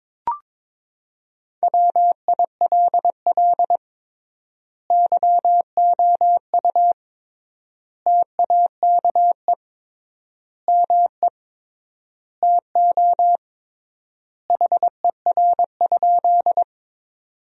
Morse code, heard as a keyed tone, about 22 wpm, 700 Hz. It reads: WILL YOU TAKE ME TO HER?
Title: 3x Word-Spacing - Sentences from Top 200 Words - 22wpm